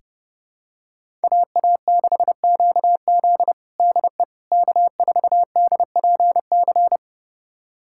Morse code, heard as a keyed tone, about 30 wpm, 700 Hz. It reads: AA6QZ DE K4DPC
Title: USA Callsign DE Callsign - 5 Characters - 30wpm